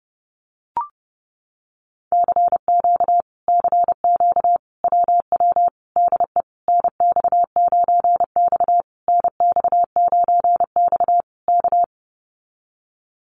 Morse code, heard as a keyed tone, about 30 wpm, 700 Hz. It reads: CQ CQ WW DE NX9X NX9X K
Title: Common QSO Phrases Encore 30wpm